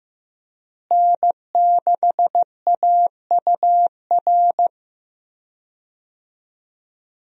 Morse code, heard as a keyed tone, about 16 wpm, 700 Hz. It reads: N6AUR
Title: USA Call Signs - 5 Characters - Repeated 3x - 15wpm